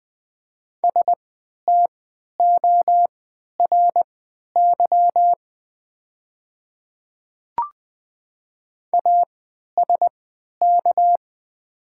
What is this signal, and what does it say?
Morse code, heard as a keyed tone, about 20 wpm, spaced as for 11 wpm, 700 Hz. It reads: STORY ASK
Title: 3x Character Spacing - Top 300 Words - 20wpm